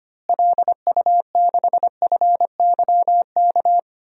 Morse code, heard as a keyed tone, about 25 wpm, 700 Hz. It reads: LU6FYK
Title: DXCC-100 Call Signs - 6 Characters - Repeated 3x - 25wpm